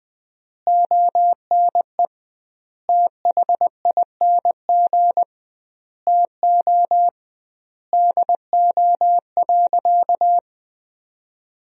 Morse code, heard as a keyed tone, about 21 wpm, 700 Hz. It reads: ONE THING TO DO.
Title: 2x Word-Spacing - Sentences from Top 100 Words - 20wpm